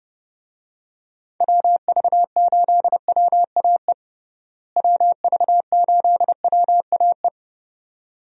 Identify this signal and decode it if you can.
Morse code, heard as a keyed tone, about 32 wpm, 700 Hz. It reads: WV8WAE WV8WAE
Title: USA Call Signs - 6 Characters - Repeated 2x - 30wpm